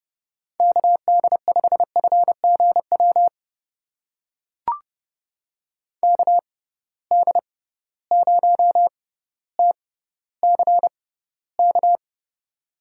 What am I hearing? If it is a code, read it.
Morse code, heard as a keyed tone, about 30 wpm, spaced as for 15 wpm, 700 Hz. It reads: KD5FGW KD0TCK
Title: USA Call Signs - 6 Characters - 6x Character Spacing - 30wpm